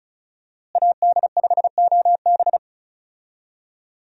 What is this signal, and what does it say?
Morse code, heard as a keyed tone, about 36 wpm, 700 Hz. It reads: AD5OB
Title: USA Call Signs - 5 Characters - Repeated 2x - 35wpm